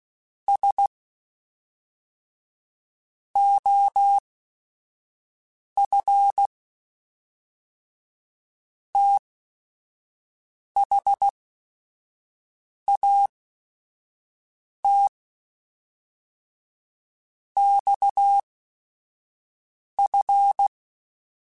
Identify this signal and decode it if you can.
Morse code, heard as a keyed tone, about 16 wpm, spaced as for 5 wpm, 800 Hz. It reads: S OF THAT XF
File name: Technical Subjects / Technical Subjects-mp3s / Sw Pwr Sup @5